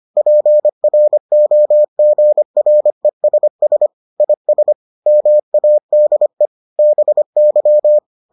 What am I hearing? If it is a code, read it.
Morse code, heard as a keyed tone, about 25 wpm, 600 Hz. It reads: PROGRESS IS MADE BY